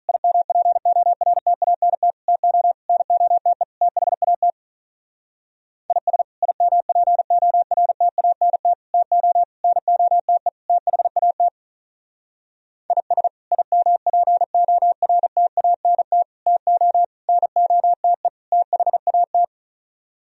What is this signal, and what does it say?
Morse code, heard as a keyed tone, about 43 wpm, 700 Hz. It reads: IMPORTANT TO NOTE THAT IS IMPORTANT TO NOTE THAT IS IMPORTANT TO NOTE THAT
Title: Speed-Racing - Sets of 5 Words 35wpm